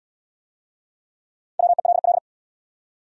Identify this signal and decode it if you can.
Morse code, heard as a keyed tone, about 67 wpm, 700 Hz. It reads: WELL